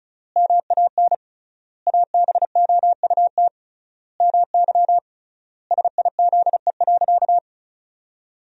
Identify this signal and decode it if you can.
Morse code, heard as a keyed tone, about 34 wpm, 700 Hz. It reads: MAN ABOUT MY SIZE.
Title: 3x Word-Spacing - Sentences from Top 400 Words - 35wpm